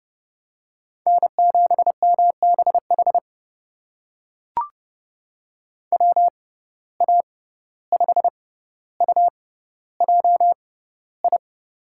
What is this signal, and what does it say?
Morse code, heard as a keyed tone, about 30 wpm, spaced as for 15 wpm, 700 Hz. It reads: N7MBH WA5UJI
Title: USA Call Signs - 6x Character Spacing - 30wpm